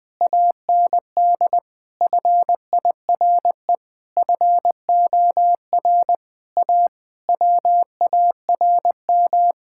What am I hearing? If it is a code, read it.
Morse code, heard as a keyed tone, about 21 wpm, 700 Hz. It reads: AND FIRE FOR A WARM
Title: Sentences from Top 1600 Words 20wpm